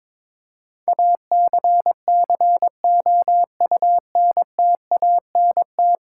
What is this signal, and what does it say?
Morse code, heard as a keyed tone, about 22 wpm, 700 Hz. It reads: ACCOUNTANT
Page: morse-code-ninja-practice-22wpm